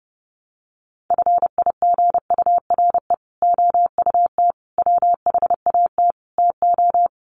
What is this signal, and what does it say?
Morse code, heard as a keyed tone, about 30 wpm, 700 Hz. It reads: FIGURE OUT WHAT TO
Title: Sets of 5 Words Encore2 30wpm